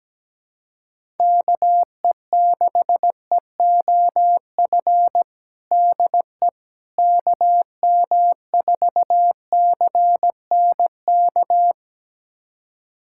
Morse code, heard as a keyed tone, about 17 wpm, 700 Hz. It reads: KE6EOF DE KM4CNK
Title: USA Callsign DE Callsign - 6 Characters - 17wpm